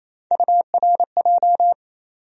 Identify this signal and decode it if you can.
Morse code, heard as a keyed tone, about 28 wpm, 700 Hz. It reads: URJ
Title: USA Call Sign Postfixes - 3 Letters 28wpm